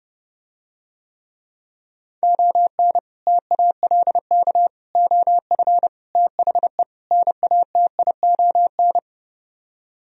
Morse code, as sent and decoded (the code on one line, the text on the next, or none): --- -. / - .- .-.. -.- / --- ..-. / - .... / -. .- - .. --- -.
ON TALK OF THE NATION